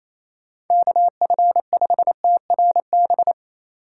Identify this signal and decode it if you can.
Morse code, heard as a keyed tone, about 28 wpm, 700 Hz. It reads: KF5TRB
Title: USA Call Signs - 6 Characters - 3x Character Spacing - 28wpm